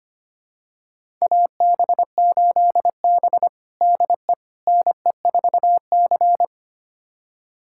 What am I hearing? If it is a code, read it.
Morse code, heard as a keyed tone, about 25 wpm, 700 Hz. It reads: AB8B DE NE4C
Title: USA Callsign DE Callsign - 4 Characters - 25wpm